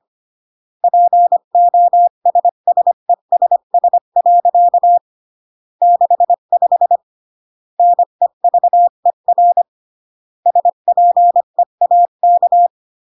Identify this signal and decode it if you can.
Morse code, heard as a keyed tone, about 25 wpm, 700 Hz. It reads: POSSESS. 65 NEVER SPEAK